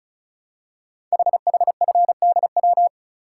Morse code, heard as a keyed tone, about 36 wpm, 700 Hz. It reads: HHFDW
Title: Code Groups - 5 Letters - 35wpm